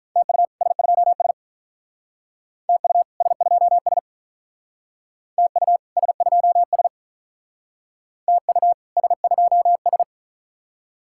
Morse code, heard as a keyed tone, about 44 wpm, 700 Hz. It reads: TU S2S TU S2S TU S2S TU S2S